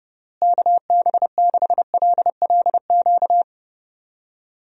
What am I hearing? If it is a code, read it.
Morse code, heard as a keyed tone, about 32 wpm, 700 Hz. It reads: KB6LLQ
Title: USA Call Signs - 6 Characters - Repeated 3x - 30wpm